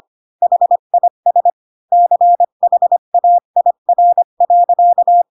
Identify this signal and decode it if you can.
Morse code, heard as a keyed tone, about 25 wpm, 700 Hz. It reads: HIS CHAIR.
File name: Pinocchio / Pinocchio ch8 20 25 ik1hge